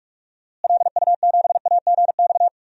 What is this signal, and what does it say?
Morse code, heard as a keyed tone, about 42 wpm, 700 Hz. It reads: LU7AGX
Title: DXCC-100 Call Signs - 6 Characters - Repeated 3x - 45wpm